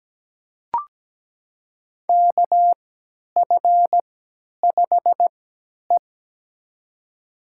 Morse code, heard as a keyed tone, about 17 wpm, spaced as for 10 wpm, 700 Hz. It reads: KF5E
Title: USA Call Signs - 4 Characters - 3x Character Spacing - 17wpm